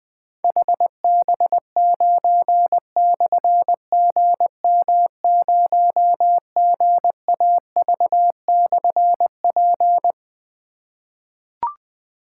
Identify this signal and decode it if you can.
Morse code, heard as a keyed tone, about 20 wpm, 700 Hz. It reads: HB9/GM0GAV/P
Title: Difficult Call Signs 20wpm